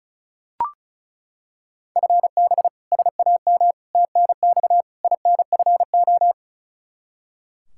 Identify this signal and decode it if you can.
Morse code, heard as a keyed tone, about 36 wpm, 700 Hz. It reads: FB SAM TNX INFO
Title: Practice QSO - Exchange 3 - More Info - 35wpm